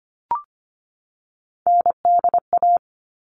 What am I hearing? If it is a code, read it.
Morse code, heard as a keyed tone, about 25 wpm, 700 Hz. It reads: NDA
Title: All Cut Numbers - 3-Digit 25wpm